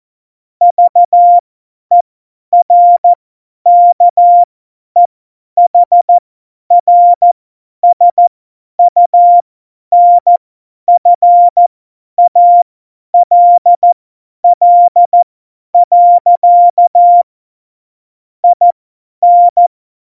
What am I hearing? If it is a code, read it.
Morse code, heard as a keyed tone, about 14 wpm, spaced as for 10 wpm, 700 Hz. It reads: VERKEHRSUNFALL. IN
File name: Morsekurs / Witze / Witz 29. September 2020, 50 F70